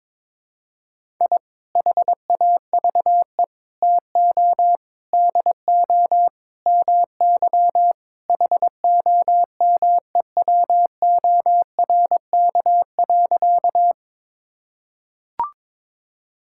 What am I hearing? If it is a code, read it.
Morse code, heard as a keyed tone, about 21 wpm, 700 Hz. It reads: I HAVE TO DO MY HOMEWORK.
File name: practice / 22wpm / Sentences with 3-Word Phrases - 22wpm